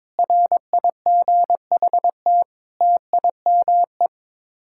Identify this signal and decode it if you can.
Morse code, heard as a keyed tone, about 22 wpm, 700 Hz. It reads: RIGHT TIME
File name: practice / 22wpm / Sets of 8 Words - 22wpm